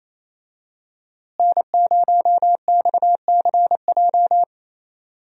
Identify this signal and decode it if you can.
Morse code, heard as a keyed tone, about 28 wpm, 700 Hz. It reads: N0XCJ
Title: USA Call Signs - 5 Characters - Repeated 3x - 28wpm